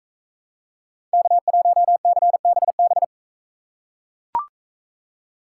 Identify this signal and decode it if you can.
Morse code, heard as a keyed tone, about 42 wpm, 700 Hz. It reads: K1CBB